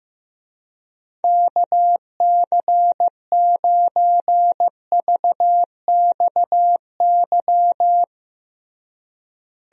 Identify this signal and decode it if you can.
Morse code, heard as a keyed tone, about 15 wpm, 700 Hz. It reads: KC9VXY
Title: USA Call Signs - 6 Characters - Repeated 2x - 15wpm